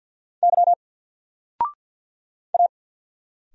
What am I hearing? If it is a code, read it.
Morse code, heard as a keyed tone, about 50 wpm, 700 Hz. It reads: Y A
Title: Single Letter-Number 50wpm